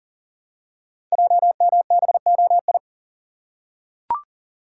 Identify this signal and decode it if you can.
Morse code, heard as a keyed tone, about 40 wpm, 700 Hz. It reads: JMBOI
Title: Code Groups - 5 Letters - 40wpm